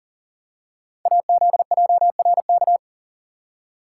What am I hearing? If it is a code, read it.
Morse code, heard as a keyed tone, about 42 wpm, 700 Hz. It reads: AZJRK
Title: Code Groups - 5 Letter:Numbers - 40wpm